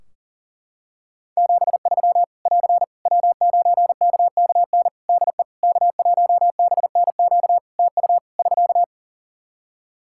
Morse code, heard as a keyed tone, about 40 wpm, 700 Hz. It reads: 73 <AR> W9KKN DE K1BNQ TU <SK>
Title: K1USN SST QSO Practice 40wpm